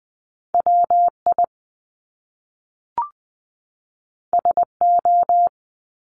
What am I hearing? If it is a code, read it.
Morse code, heard as a keyed tone, about 20 wpm, 700 Hz. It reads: WI SO
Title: Top Two Letter Combinations in Words 20wpm